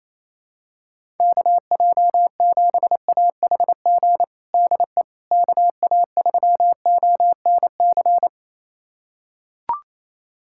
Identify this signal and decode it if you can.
Morse code, heard as a keyed tone, about 28 wpm, 700 Hz. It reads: KJ7AHG DE KA3ONC
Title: USA Callsign DE Callsign 28wpm